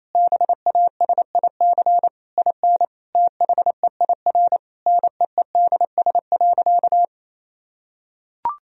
Morse code, heard as a keyed tone, about 28 wpm, 700 Hz. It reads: BASIC IN THEIR NEEDS.
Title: Sentences from Top 1000 Words - Review Part 3 - 28wpm